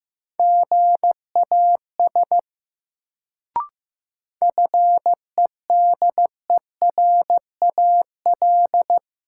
Morse code, heard as a keyed tone, about 16 wpm, 700 Hz. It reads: GAS FEDERAL
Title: Sentences from Top 1100 Words - New Word Drill Alt - 15wpm